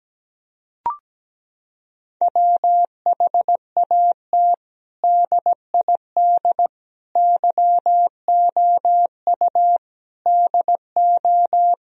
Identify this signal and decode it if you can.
Morse code, heard as a keyed tone, about 17 wpm, 700 Hz. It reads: WHAT DID YOU DO
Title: Sentences from Top 1000 Words - Review Part 1 - 17wpm